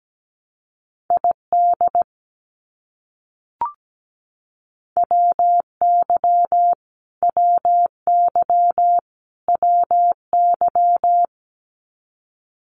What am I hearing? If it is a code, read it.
Morse code, heard as a keyed tone, about 17 wpm, 700 Hz. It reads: ID WY WY WY
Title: US States Repeated 3x 17wpm